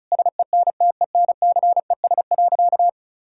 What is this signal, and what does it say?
Morse code, heard as a keyed tone, about 36 wpm, 700 Hz. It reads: SENTENCES.
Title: Pangram Challenge v2 - 35wpm